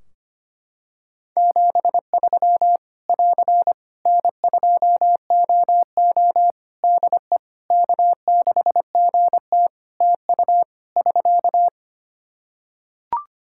Morse code, heard as a keyed tone, about 25 wpm, 700 Hz. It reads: 73 <AR> N2OO DE K6GT TU <SK>
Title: Practice QSO - Exchange 4 - Ending - 25wpm